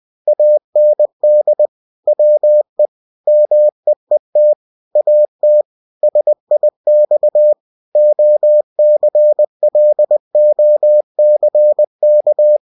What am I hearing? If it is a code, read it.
Morse code, heard as a keyed tone, about 20 wpm, 600 Hz. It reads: AND WE MEET AT SIX OCLOCK
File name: cw-books / winnie-the-pooh / winnie-20wpm-page-0038